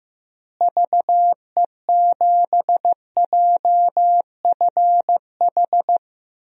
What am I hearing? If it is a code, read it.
Morse code, heard as a keyed tone, about 16 wpm, 700 Hz. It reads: VE7JFH
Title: DXCC-100 Call Signs - 6 Characters - Repeated 2x - 15wpm